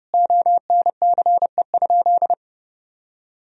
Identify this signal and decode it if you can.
Morse code, heard as a keyed tone, about 32 wpm, 700 Hz. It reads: ONCE?
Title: Sentences from Top 1000 Words - Review Part 1 - 30wpm